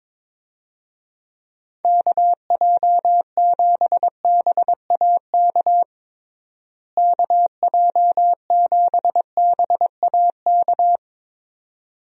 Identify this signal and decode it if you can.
Morse code, heard as a keyed tone, about 22 wpm, 700 Hz. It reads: KJ7BAK KJ7BAK
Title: USA Call Signs - 6 Characters - Repeated 2x - 22wpm